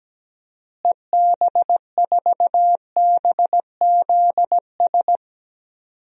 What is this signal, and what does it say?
Morse code, heard as a keyed tone, about 17 wpm, 700 Hz. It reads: EB4BZS
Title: DXCC-100 Call Signs - 6 Characters - Repeated 3x - 17wpm